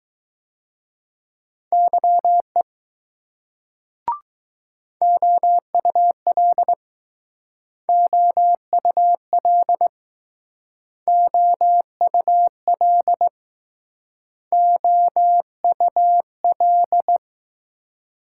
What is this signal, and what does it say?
Morse code, heard as a keyed tone, about 20 wpm, 700 Hz. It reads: YE OUL OUL OUL OUL